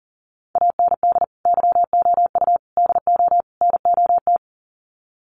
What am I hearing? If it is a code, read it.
Morse code, heard as a keyed tone, about 40 wpm, 700 Hz. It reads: AND YOU DO NOT
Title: Sets of 4 Words 40wpm